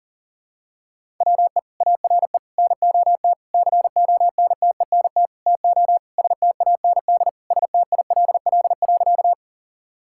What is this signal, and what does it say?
Morse code, heard as a keyed tone, about 40 wpm, 700 Hz. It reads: WE ARE NOT CONTENT TO STAND STILL.
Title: Sentences from Top 1000 Words - Review Part 6 - 40wpm